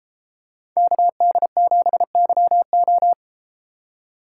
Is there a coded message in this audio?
Yes: Morse code from 0.8 to 3.1 s, heard as a keyed tone, about 32 wpm, 700 Hz. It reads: KD7YO